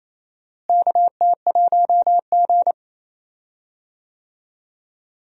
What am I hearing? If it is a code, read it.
Morse code, heard as a keyed tone, about 28 wpm, 700 Hz. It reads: KT1G
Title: USA Call Signs - 4 Characters - Repeated 2x - 28wpm